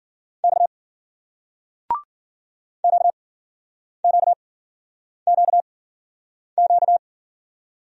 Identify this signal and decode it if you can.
Morse code, heard as a keyed tone, about 53 wpm, spaced as for 9 wpm, 700 Hz. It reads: X QQQQ